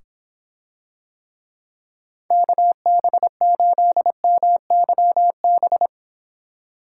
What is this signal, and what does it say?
Morse code, heard as a keyed tone, about 26 wpm, 700 Hz. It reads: KB8MYB